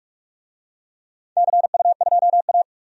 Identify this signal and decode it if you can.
Morse code, heard as a keyed tone, about 46 wpm, 700 Hz. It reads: CU2A
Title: DXCC-100 Call Signs - 4 Characters - 45wpm